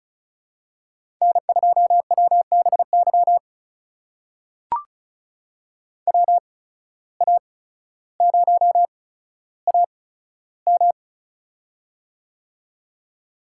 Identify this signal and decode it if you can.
Morse code, heard as a keyed tone, about 36 wpm, spaced as for 15 wpm, 700 Hz. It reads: N2WBY WA0AM
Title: USA Call Signs - 5 Characters - 8x Character Spacing - 35wpm